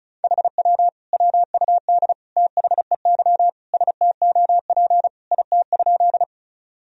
Morse code, heard as a keyed tone, about 36 wpm, 700 Hz. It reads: HW WUD THEY STOP IT?
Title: Sentences from Top 300 Words and Abbreviations 35wpm